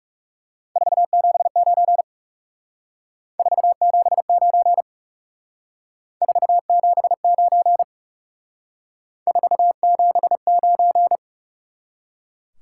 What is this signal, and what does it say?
Morse code, heard as a keyed tone, about 36 wpm, 700 Hz. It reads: 479 479 479 479
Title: Speed-Racing Top 180 QSO Elements 30wpm